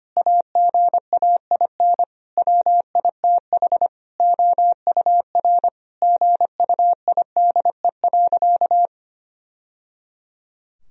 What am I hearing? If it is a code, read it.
Morse code, heard as a keyed tone, about 25 wpm, 700 Hz. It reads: AGAIN WITH OUR GUIDE.
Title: Sentences from Top 1000 Words - Review Part 3 - 25wpm